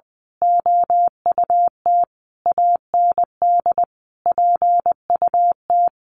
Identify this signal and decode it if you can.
Morse code, heard as a keyed tone, about 20 wpm, 700 Hz. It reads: OUT AND PUT
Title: A Sound Of Thunder 20wpm